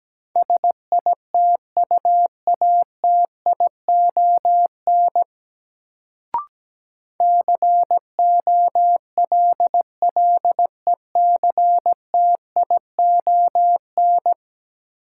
Morse code, heard as a keyed tone, about 17 wpm, 700 Hz. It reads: SITUATION COLLECTION
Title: Sentences from Top 1200 Words - New Word Drill Alt - 17wpm